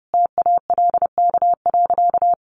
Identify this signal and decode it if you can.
Morse code, heard as a keyed tone, about 28 wpm, 700 Hz. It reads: TALK.
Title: Sentences from Top 1300 Words 30wpm